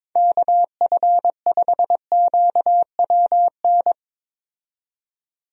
Morse code, heard as a keyed tone, about 22 wpm, 700 Hz. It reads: KF5QWN
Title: USA Call Signs - 6 Characters - 22wpm